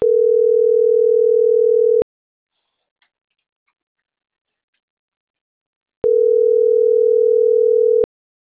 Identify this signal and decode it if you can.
Ringback tone (North America), 440 + 480 Hz.